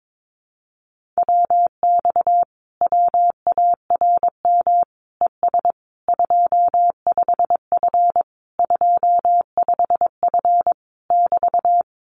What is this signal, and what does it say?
Morse code, heard as a keyed tone, about 22 wpm, 700 Hz. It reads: WX WARM ES 25F 25F <BT>
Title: Common QSO Phrases Encore 22wpm